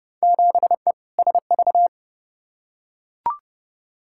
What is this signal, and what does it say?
Morse code, heard as a keyed tone, about 30 wpm, 700 Hz. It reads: 7E SV
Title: Field Day Exchanges Encore - 30wpm